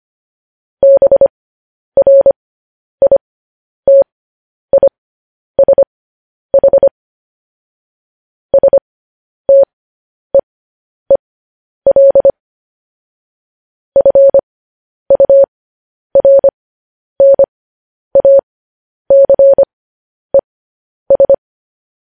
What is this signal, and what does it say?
Morse code, heard as a keyed tone, about 25 wpm, spaced as for 10 wpm, 550 Hz. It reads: BRITISH STEEL FURNACES
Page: News Headlines in Morse Code at 10 WPM